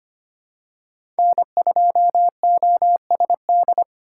Morse code, heard as a keyed tone, about 25 wpm, 700 Hz. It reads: N2OSD
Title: USA Call Signs - 5 Characters - 3x Character Spacing - 25wpm